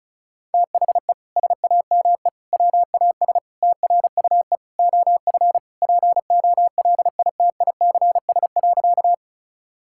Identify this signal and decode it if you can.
Morse code, heard as a keyed tone, about 36 wpm, 700 Hz. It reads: THE SAME WAS TRUE OF POLITICS.
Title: Sentences from Top 1000 Words - Review Part 6 - 35wpm